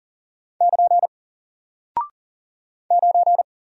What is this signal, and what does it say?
Morse code, heard as a keyed tone, about 40 wpm, 700 Hz. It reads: <KN> 9
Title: Single Character (Letter:Number:Punctuation:Prosign) Alt 40wpm